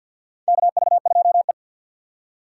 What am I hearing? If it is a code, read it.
Morse code, heard as a keyed tone, about 53 wpm, 700 Hz. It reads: KV2E
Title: USA Call Signs - 4 Characters - 2x Character Spacing - 50wpm